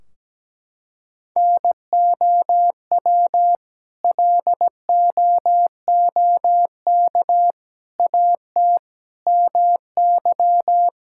Morse code, heard as a keyed tone, about 17 wpm, 700 Hz. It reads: NOW LOOK AT MY
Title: Sentences from Top 1600 Words 17wpm